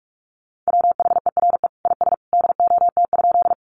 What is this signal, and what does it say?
Morse code, heard as a keyed tone, about 44 wpm, 700 Hz. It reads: WHERE IS DOT?
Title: Sentences from Top 1500 Words 45wpm